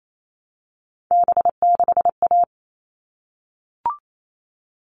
Morse code, heard as a keyed tone, about 28 wpm, 700 Hz. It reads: B6A
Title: All Cut Numbers - 3-Digit 28wpm